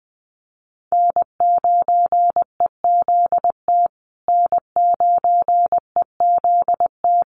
Morse code, heard as a keyed tone, about 20 wpm, 700 Hz. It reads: N9EZT N9EZT
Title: USA Call Signs Repeated 3x 20wpm